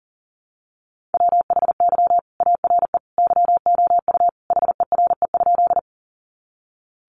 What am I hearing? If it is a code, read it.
Morse code, heard as a keyed tone, about 40 wpm, 700 Hz. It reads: WHY ARE YOU HERE?
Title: Sentences from Top 200 Words 40wpm